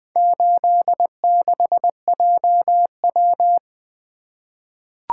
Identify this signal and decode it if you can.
Morse code, heard as a keyed tone, about 20 wpm, 700 Hz. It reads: 86JW
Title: Four Letter-Number Combinations - 20wpm